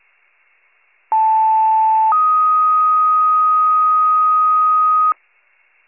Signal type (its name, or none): two-tone page